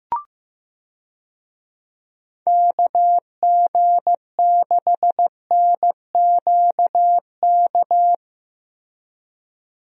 Morse code, heard as a keyed tone, about 15 wpm, 700 Hz. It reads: KG6NQK
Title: USA Call Signs - 6 Characters - Repeated 2x - 15wpm